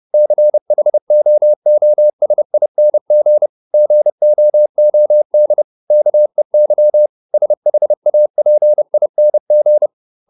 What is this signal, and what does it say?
Morse code, heard as a keyed tone, about 30 wpm, 600 Hz. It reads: CHOOSING GOOD KEY SHAPING